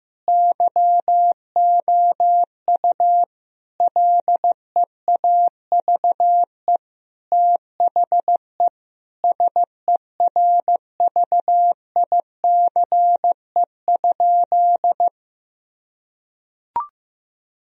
Morse code, heard as a keyed tone, about 15 wpm, 700 Hz. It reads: YOU LEAVE THE SERVICE?
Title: Sentences from Top 1000 Words - Review Part 4 - 15wpm